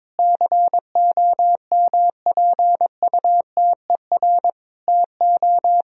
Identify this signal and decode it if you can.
Morse code, heard as a keyed tone, about 22 wpm, 700 Hz. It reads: COMPUTER TO
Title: Sentences from Top 1000 Words - Review Part 3 - 22wpm